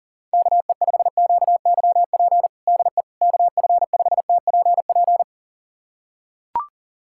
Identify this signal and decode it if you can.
Morse code, heard as a keyed tone, about 40 wpm, 700 Hz. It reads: KE5QYP DE KF5TPP
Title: USA Callsign DE Callsign - 6 Characters - 40wpm